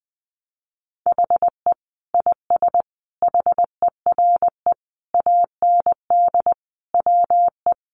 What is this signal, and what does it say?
Morse code, heard as a keyed tone, about 20 wpm, 700 Hz. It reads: HE IS HERE AND WE
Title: Sentences from Top 800 Words 20wpm